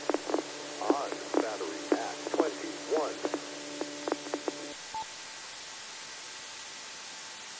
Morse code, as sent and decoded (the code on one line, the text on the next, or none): none